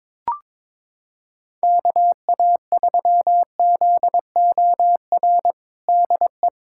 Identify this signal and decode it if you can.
Morse code, heard as a keyed tone, about 21 wpm, 700 Hz. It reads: KA3ZOR DE
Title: USA Callsign DE Callsign - 6 Characters - 22wpm